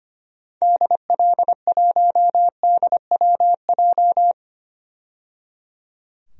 Morse code, heard as a keyed tone, about 25 wpm, 700 Hz. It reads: DL1DWJ
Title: DXCC-100 Call Signs - 6 Characters - 25wpm